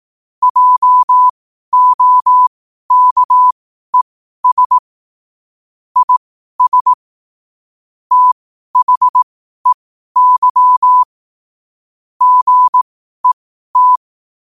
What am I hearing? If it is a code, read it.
Morse code, heard as a keyed tone, about 18 wpm, spaced as for 12 wpm, 1000 Hz. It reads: JOKES IS THEY GET
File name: qotdfiles / 12wpm